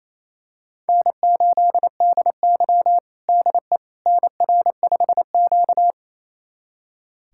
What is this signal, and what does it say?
Morse code, heard as a keyed tone, about 28 wpm, 700 Hz. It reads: N8DY DE NR5Q
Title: USA Callsign DE Callsign - 4 Characters - 28wpm